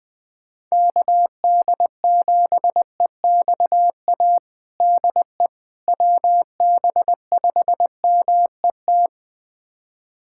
Morse code, heard as a keyed tone, about 20 wpm, 700 Hz. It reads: KD7EXA DE WB5MET
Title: USA Callsign DE Callsign - 6 Characters - 20wpm